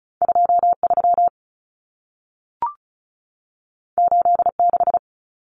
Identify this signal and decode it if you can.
Morse code, heard as a keyed tone, about 36 wpm, 700 Hz. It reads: 23 86
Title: Common Cut Numbers - 2-Digit 35wpm